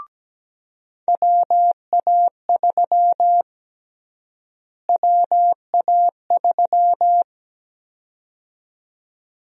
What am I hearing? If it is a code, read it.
Morse code, heard as a keyed tone, about 17 wpm, 700 Hz. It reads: WA3 WA3
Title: USA Call Sign Prefixes - Repeated 2x - 17wpm